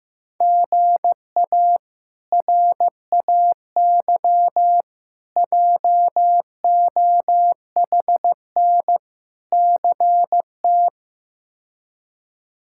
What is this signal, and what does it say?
Morse code, heard as a keyed tone, about 15 wpm, 700 Hz. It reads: GA RAY JOHN CT
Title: K1USN SST QSO Practice - 15wpm